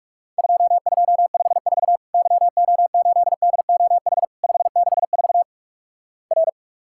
Morse code, heard as a keyed tone, about 48 wpm, 700 Hz. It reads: 2254 YO8DOH 564 R
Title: Worked All Europe DX Contest - QTC Exchange - 45wpm